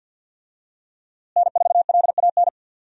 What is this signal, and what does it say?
Morse code, heard as a keyed tone, about 50 wpm, 700 Hz. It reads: N4LAN
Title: USA Call Signs - 5 Characters - Repeated 2x - 50wpm